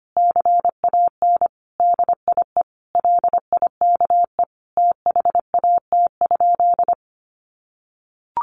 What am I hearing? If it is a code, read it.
Morse code, heard as a keyed tone, about 25 wpm, 700 Hz. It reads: CAN DIE LIKE THAT?
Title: Sentences from Top 1400 Words 25wpm